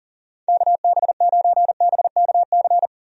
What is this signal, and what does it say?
Morse code, heard as a keyed tone, about 40 wpm, 700 Hz. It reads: KB9BKC
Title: USA Call Signs - 6 Characters - 40wpm